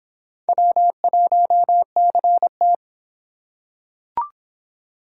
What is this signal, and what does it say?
Morse code, heard as a keyed tone, about 25 wpm, 700 Hz. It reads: W1CT